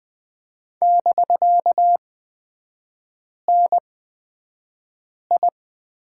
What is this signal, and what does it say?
Morse code, heard as a keyed tone, about 20 wpm, spaced as for 5 wpm, 700 Hz. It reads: <BK>NI
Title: 250 Lesson 40e - BK - Three Characters 20wpm